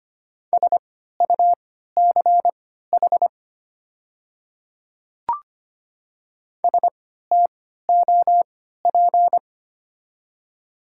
Morse code, heard as a keyed tone, about 25 wpm, spaced as for 14 wpm, 700 Hz. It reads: SUCH STOP
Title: 3x Character Spacing - Top 300 Words - 25wpm